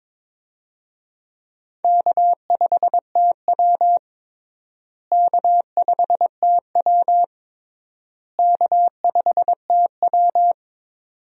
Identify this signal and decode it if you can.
Morse code, heard as a keyed tone, about 22 wpm, 700 Hz. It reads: K5TW K5TW K5TW
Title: USA Call Signs - 4 Characters - Repeated 3x - 22wpm